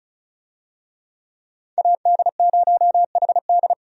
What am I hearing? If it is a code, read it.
Morse code, heard as a keyed tone, about 34 wpm, 700 Hz. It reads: AD0HD
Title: USA Call Signs - 5 Characters - 35wpm